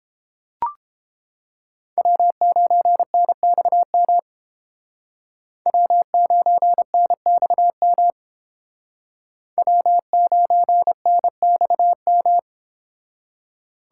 Morse code, heard as a keyed tone, about 29 wpm, 700 Hz. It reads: W9NXM W9NXM W9NXM